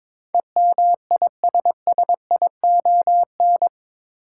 Morse code, heard as a keyed tone, about 21 wpm, 700 Hz. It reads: EMISSION
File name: practice / 22wpm / Eight Letter Words - 22wpm